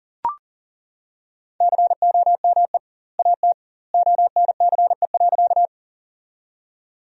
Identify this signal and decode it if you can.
Morse code, heard as a keyed tone, about 40 wpm, 700 Hz. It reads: COME AT ONCE.
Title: 2x Word-Spacing - Sentences from Top 300 Words - 40wpm